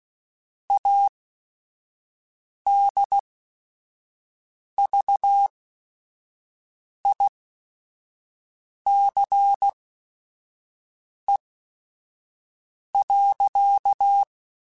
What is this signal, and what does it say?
Morse code, heard as a keyed tone, about 16 wpm, spaced as for 5 wpm, 800 Hz. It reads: ADVICE.